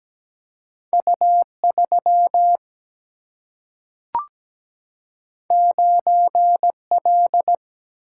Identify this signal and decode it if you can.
Morse code, heard as a keyed tone, about 17 wpm, 700 Hz. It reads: U3 9L
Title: Single-Letter Single-Number 17wpm